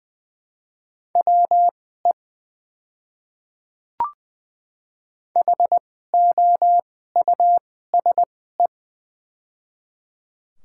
Morse code, heard as a keyed tone, about 20 wpm, spaced as for 14 wpm, 700 Hz. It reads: WE HOUSE
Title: 2x Character Spacing - Top 300 Words - 20wpm